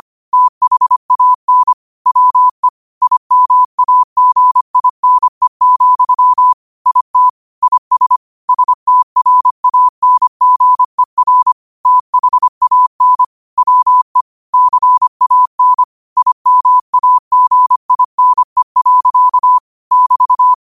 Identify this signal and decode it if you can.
Morse code, heard as a keyed tone, about 25 wpm, 1000 Hz. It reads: THAN WE IMAGINE, IT IS STRANGER THAN WE CAN IMAGINE. <BT>